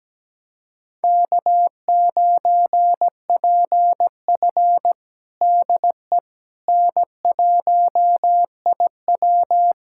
Morse code, heard as a keyed tone, about 17 wpm, 700 Hz. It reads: K9PF DE N1IW
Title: USA Callsign DE Callsign - 4 Characters - 17wpm